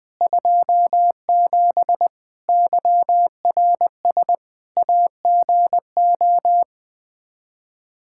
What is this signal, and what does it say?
Morse code, heard as a keyed tone, about 20 wpm, 700 Hz. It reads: 27 YRS AGO
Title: K1USN SST QSO Practice 20wpm